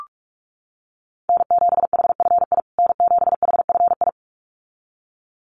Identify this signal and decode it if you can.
Morse code, heard as a keyed tone, about 44 wpm, 700 Hz. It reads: N7HFI N7HFI
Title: USA Call Signs Repeated 2x 45wpm